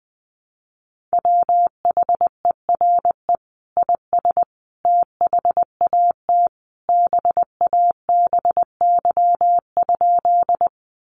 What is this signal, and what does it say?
Morse code, heard as a keyed tone, about 20 wpm, 700 Hz. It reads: WHERE IS THAT BABY?
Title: Sentences from Top 1300 Words 20wpm